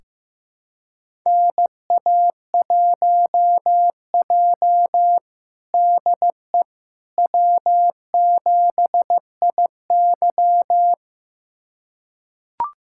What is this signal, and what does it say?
Morse code, heard as a keyed tone, about 15 wpm, 700 Hz. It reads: NA1J DE W7IY
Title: USA Callsign DE Callsign - 4 Characters - 15wpm